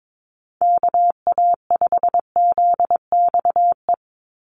Morse code, heard as a keyed tone, about 22 wpm, 700 Hz. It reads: KA5ZXE
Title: USA Call Signs 22wpm